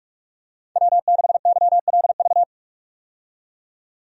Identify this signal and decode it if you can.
Morse code, heard as a keyed tone, about 46 wpm, 700 Hz. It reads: W6YLV
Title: USA Call Signs - 5 Characters - 45wpm